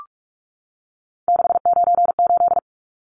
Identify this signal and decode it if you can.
Morse code, heard as a keyed tone, about 44 wpm, 700 Hz. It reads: 698
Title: Numbers - 3-Digit 45wpm